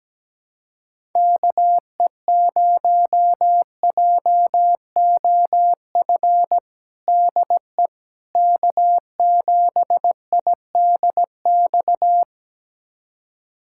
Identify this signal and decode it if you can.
Morse code, heard as a keyed tone, about 17 wpm, 700 Hz. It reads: KE0JOF DE K7IDX
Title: USA Callsign DE Callsign 17wpm